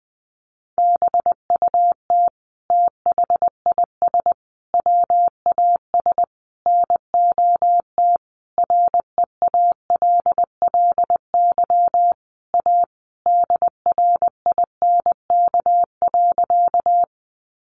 Morse code, heard as a keyed tone, about 20 wpm, 700 Hz. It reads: BUT THIS WAS NOT REALLY A DRINK.
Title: Sentences from Top 1500 Words 20wpm